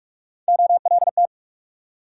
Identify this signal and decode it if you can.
Morse code, heard as a keyed tone, about 56 wpm, 690 Hz. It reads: OLT
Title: Speed-Racing Multi-letter Phonemes - 45wpm